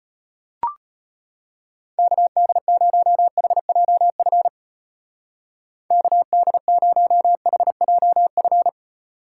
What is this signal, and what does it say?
Morse code, heard as a keyed tone, about 31 wpm, 700 Hz. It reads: KD0HJF KD0HJF
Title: Speed Racing - USA 2x3 Call Signs 25wpm